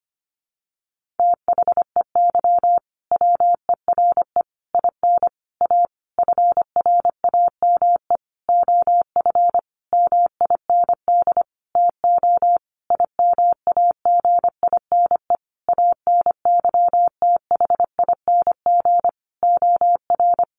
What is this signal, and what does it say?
Morse code, heard as a keyed tone, about 25 wpm, 700 Hz. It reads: THEY WERE IN A FRAME OF MIND TO IMAGINE ANYTHING OR
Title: Sentences from Top 1400 Words 25wpm